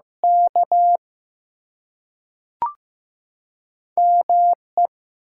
Morse code, heard as a keyed tone, about 15 wpm, 700 Hz. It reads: K ME
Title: Two Letter Words - 15wpm